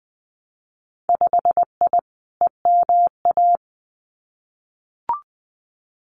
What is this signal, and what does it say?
Morse code, heard as a keyed tone, about 20 wpm, 700 Hz. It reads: 5I EMA
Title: Winter Field Day Exchanges Encore 20wpm